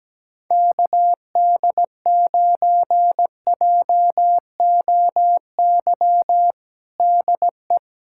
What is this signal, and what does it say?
Morse code, heard as a keyed tone, about 17 wpm, 700 Hz. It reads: KD9JOY DE
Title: USA Callsign DE Callsign - 6 Characters - 17wpm